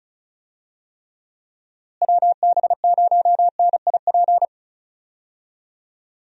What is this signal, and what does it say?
Morse code, heard as a keyed tone, about 36 wpm, 700 Hz. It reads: WB0NIP
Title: USA Call Signs - 6 Characters - Repeated 2x - 35wpm